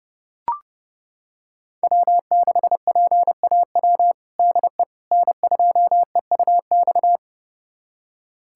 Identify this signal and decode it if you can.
Morse code, heard as a keyed tone, about 30 wpm, 700 Hz. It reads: W6PAW DE N2EUX
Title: USA Callsign DE Callsign - 5 Characters - 30wpm